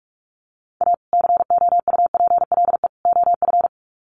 Morse code, heard as a keyed tone, about 44 wpm, 700 Hz. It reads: A COUPLE OF
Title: Sets of 3 Words 45wpm